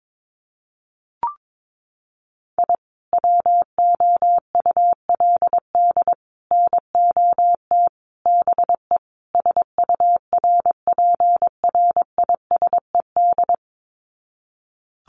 Morse code, heard as a keyed tone, about 22 wpm, 700 Hz. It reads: I WOULD NOT BE SURPRISED
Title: Sets of 5 Words Encore2 22wpm